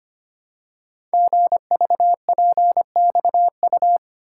Morse code, heard as a keyed tone, about 25 wpm, 700 Hz. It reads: GVPXU
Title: Code Groups - 5 Letters - 25wpm